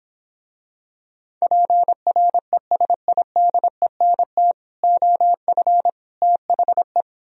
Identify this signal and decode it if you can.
Morse code, heard as a keyed tone, about 26 wpm, 700 Hz. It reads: PRESIDENT OF THE